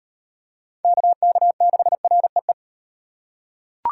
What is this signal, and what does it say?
Morse code, heard as a keyed tone, about 36 wpm, 700 Hz. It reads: KK6REE